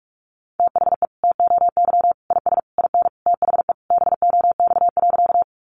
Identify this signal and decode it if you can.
Morse code, heard as a keyed tone, about 44 wpm, 700 Hz. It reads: THE TOY IS IN THE BOX.
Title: Easy Phrases 45wpm